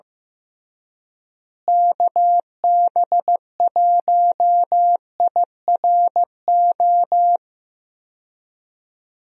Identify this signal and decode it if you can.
Morse code, heard as a keyed tone, about 15 wpm, 700 Hz. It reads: KB1IRO
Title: USA Call Signs - 6 Characters - Repeated 3x - 15wpm